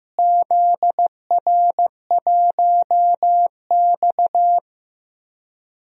Morse code, heard as a keyed tone, about 15 wpm, 700 Hz. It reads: ZR1X
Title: DXCC-100 Call Signs - 4 Characters - Repeated 3x - 15wpm